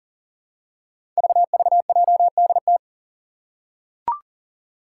Morse code, heard as a keyed tone, about 40 wpm, 700 Hz. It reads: VVJDT
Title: Five Challenging Letter-Number Combinations 40wpm